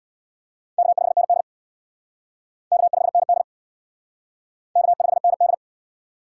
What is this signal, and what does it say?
Morse code, heard as a keyed tone, about 60 wpm, 700 Hz. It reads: K5ND K5ND K5ND
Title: Speed Racing - USA Call Signs 50wpm